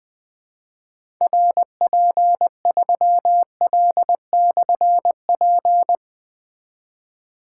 Morse code, heard as a keyed tone, about 20 wpm, 700 Hz. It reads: RP3L/P
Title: Difficult Call Signs 20wpm